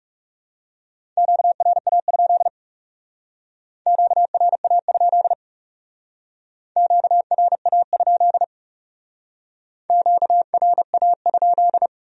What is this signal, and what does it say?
Morse code, heard as a keyed tone, about 38 wpm, 700 Hz. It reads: QRA? QRA? QRA? QRA?